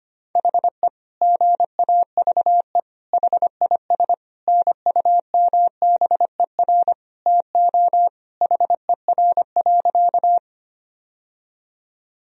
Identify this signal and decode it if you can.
Morse code, heard as a keyed tone, about 25 wpm, 700 Hz. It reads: HE GAVE HIS NUMBER TO HER.
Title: Sentences from Top 1000 Words - Review Part 1 - 25wpm